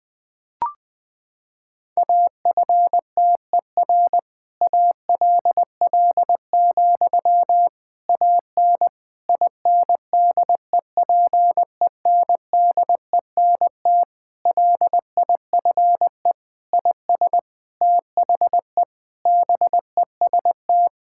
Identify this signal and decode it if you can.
Morse code, heard as a keyed tone, about 21 wpm, 700 Hz. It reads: AFTER ALL, AN INDEPENDENT LIFE IS THE BEST
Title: Sentences from Top 1000 Words - Review Part 6 - 20wpm